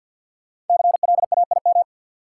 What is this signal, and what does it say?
Morse code, heard as a keyed tone, about 52 wpm, 700 Hz. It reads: CLAIM
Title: Sentences from Top 1500 Words - New Word Drill Alt - 50wpm